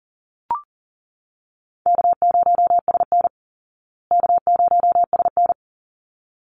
Morse code, heard as a keyed tone, about 40 wpm, 700 Hz. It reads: K0SN K0SN
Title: Call Signs Repeated 3x - 40wpm